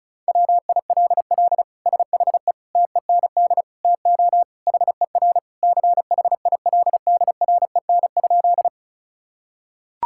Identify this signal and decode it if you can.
Morse code, heard as a keyed tone, about 34 wpm, 700 Hz. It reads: WILL SHE TEND TO HER CHILDREN?
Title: Sentences from Top 1000 Words - Review Part 5 - 35wpm